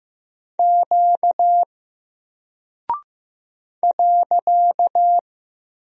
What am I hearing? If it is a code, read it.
Morse code, heard as a keyed tone, about 15 wpm, 700 Hz. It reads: Q .
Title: Single Character (Letter:Number:Punctuation:Prosign) 15wpm